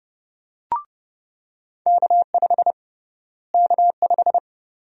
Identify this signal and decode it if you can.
Morse code, heard as a keyed tone, about 30 wpm, 700 Hz. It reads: K5 K5
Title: USA Call Sign Prefixes - Repeated 2x - 30wpm